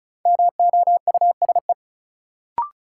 Morse code, heard as a keyed tone, about 36 wpm, 700 Hz. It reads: MOUSE
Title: Sentences from Top 700 Words - New Word Drill - 35wpm